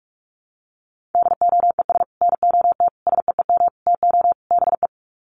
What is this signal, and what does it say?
Morse code, heard as a keyed tone, about 44 wpm, 700 Hz. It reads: DOES NOT SEEM TO BE
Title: Sets of 5 Words 45wpm